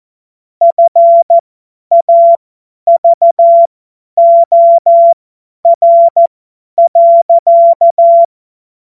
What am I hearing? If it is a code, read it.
Morse code, heard as a keyed tone, about 14 wpm, spaced as for 10 wpm, 680 Hz. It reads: FAVOR.
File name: Morsekurs / Witze / Witz 14. Dezember 2021, 50 F70